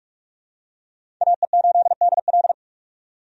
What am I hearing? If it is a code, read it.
Morse code, heard as a keyed tone, about 46 wpm, 700 Hz. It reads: AE8DL
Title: USA Call Signs - 5 Characters - 45wpm